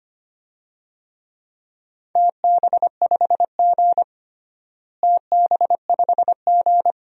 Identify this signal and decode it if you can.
Morse code, heard as a keyed tone, about 25 wpm, 700 Hz. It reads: TB5G TB5G
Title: DXCC-100 Call Signs - 4 Characters - Repeated 2x - 25wpm